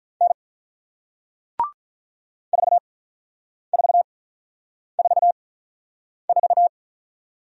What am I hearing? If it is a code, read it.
Morse code, heard as a keyed tone, about 48 wpm, spaced as for 9 wpm, 690 Hz. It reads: N 4444